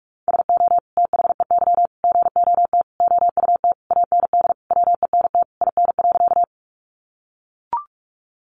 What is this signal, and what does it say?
Morse code, heard as a keyed tone, about 44 wpm, 700 Hz. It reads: SO THEY GOT OUT AND WENT IN.
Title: Sentences from Top 300 Words 45wpm